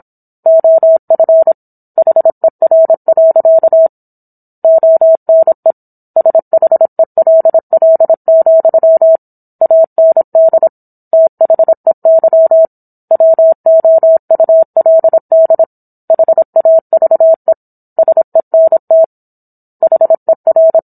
Morse code, heard as a keyed tone, about 27 wpm, 650 Hz. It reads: OF HER. ONE SHELL, AND THEY WOULD HAVE SENT HER
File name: Book One / 17-Chapter 17 - 26 WPM